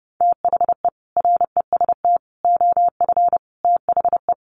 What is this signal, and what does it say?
Morse code, heard as a keyed tone, about 30 wpm, 700 Hz. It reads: THE REST OF THE
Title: Sets of 5 Words 30wpm